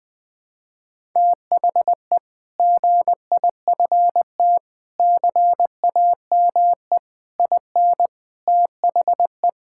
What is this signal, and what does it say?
Morse code, heard as a keyed tone, about 20 wpm, 700 Hz. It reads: THE GIFT CAME IN THE
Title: Sentences with 5-Word Phrases - 20wpm